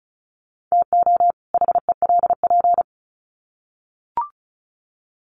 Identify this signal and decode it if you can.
Morse code, heard as a keyed tone, about 34 wpm, 700 Hz. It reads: TO HELP
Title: Sets of 2 Words 35wpm